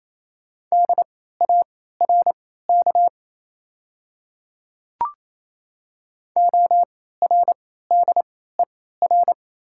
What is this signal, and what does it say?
Morse code, heard as a keyed tone, about 28 wpm, spaced as for 16 wpm, 690 Hz. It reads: DARK ORDER
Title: 3x Character Spacing - Top 500 Words - 28wpm